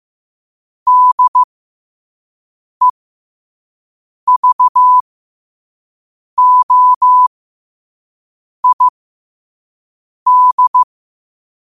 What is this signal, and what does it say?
Morse code, heard as a keyed tone, about 15 wpm, spaced as for 5 wpm, 1000 Hz. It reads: DEVOID
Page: Quotes for Wed, 13 Aug 2025 in Morse Code at 5 words per minute.